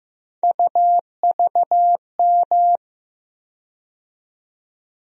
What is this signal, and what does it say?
Morse code, heard as a keyed tone, about 15 wpm, 700 Hz. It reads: UVM
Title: USA Call Sign Postfixes - 3 Letters 15wpm